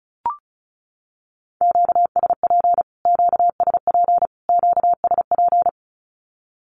Morse code, heard as a keyed tone, about 34 wpm, 700 Hz. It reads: QSP QSP QSP
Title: Q-Codes Repeated 3x 35wpm